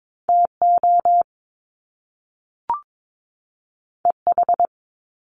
Morse code, heard as a keyed tone, about 22 wpm, 700 Hz. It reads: TO EH
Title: Top Two Letter Combinations in Words 22wpm